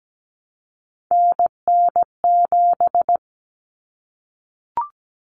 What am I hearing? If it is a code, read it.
Morse code, heard as a keyed tone, about 17 wpm, 700 Hz. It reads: NN7
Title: Common Cut Numbers - 3-Digit 17wpm